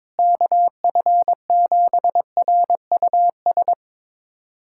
Morse code, heard as a keyed tone, about 21 wpm, 700 Hz. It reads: KF7RUS
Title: USA Callsign DE Callsign - 6 Characters - 22wpm